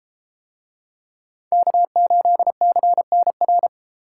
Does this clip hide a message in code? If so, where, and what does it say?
Yes: Morse code from 1.5 to 3.7 s, heard as a keyed tone, about 31 wpm, 700 Hz. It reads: K8CNR